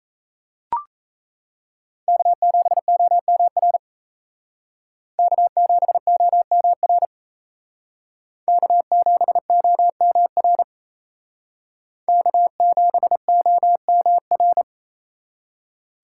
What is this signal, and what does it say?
Morse code, heard as a keyed tone, about 35 wpm, 700 Hz. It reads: K7OMR K7OMR K7OMR K7OMR
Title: Speed Racing - USA 1x3 Call Signs 28wpm